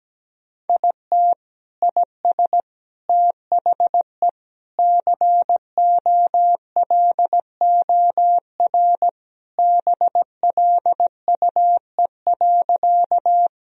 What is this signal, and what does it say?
Morse code, heard as a keyed tone, about 17 wpm, 700 Hz. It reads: IT IS THE COLOR BLUE.
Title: Sentences from Top 1000 Words - Review Part 3 - 17wpm